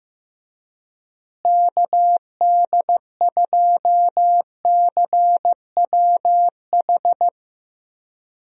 Morse code, heard as a keyed tone, about 15 wpm, 700 Hz. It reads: KD2CWH
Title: USA Call Signs - 6 Characters - Repeated 3x - 15wpm